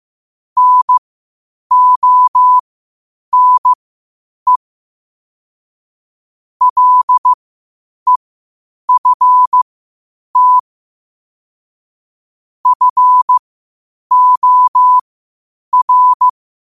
Morse code, heard as a keyed tone, about 15 wpm, spaced as for 8 wpm, 1000 Hz. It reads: NONE LEFT FOR
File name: qotdfiles / 8wpm